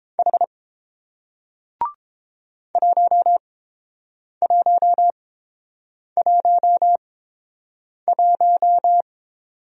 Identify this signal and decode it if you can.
Morse code, heard as a keyed tone, about 30 wpm, spaced as for 7 wpm, 700 Hz. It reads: H 1111